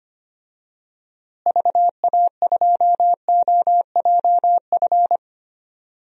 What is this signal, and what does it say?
Morse code, heard as a keyed tone, about 25 wpm, 700 Hz. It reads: VA2OJF
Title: DXCC-100 Call Signs - 6 Characters - Repeated 3x - 25wpm